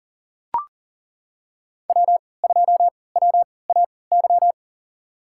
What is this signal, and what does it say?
Morse code, heard as a keyed tone, about 39 wpm, spaced as for 23 wpm, 700 Hz. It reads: W2WAY
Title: USA Call Signs - 5 Characters - 3x Character Spacing - 40wpm